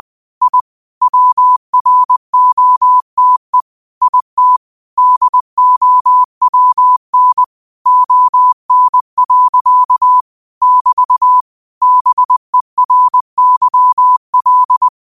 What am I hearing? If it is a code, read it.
Morse code, heard as a keyed tone, about 20 wpm, 1000 Hz. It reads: I WROTE IT DOWN ON. <BT> BERYL
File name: qotdfiles / 20wpm